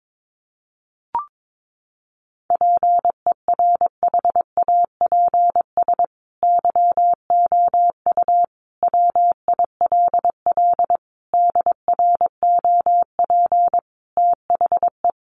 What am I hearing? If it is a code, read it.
Morse code, heard as a keyed tone, about 22 wpm, 700 Hz. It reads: PERHAPS YOU WILL DROP THE
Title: Sentences from Top 1400 Words 22wpm